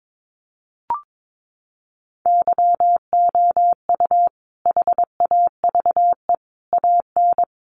Morse code, heard as a keyed tone, about 22 wpm, 700 Hz. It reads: YOU HAVE AN
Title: Sentences from Top 400 Words 22wpm